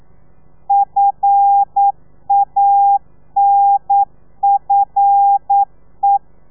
Morse code, heard as a keyed tone, about 9 wpm, 800 Hz. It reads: FANFE
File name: telegrafia / i2tfj / Lez5 40